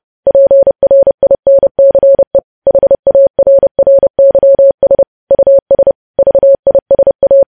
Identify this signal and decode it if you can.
Morse code, heard as a keyed tone, about 30 wpm, 550 Hz. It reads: PRINCE HARRYS US VISA
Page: news headlines in morse code at 30 wpm